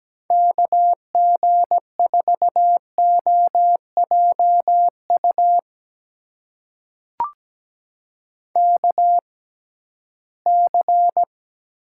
Morse code, heard as a keyed tone, about 17 wpm, spaced as for 13 wpm, 700 Hz. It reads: KG4OJU KC
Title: USA Call Signs - 6 Characters - 6x Character Spacing - 17wpm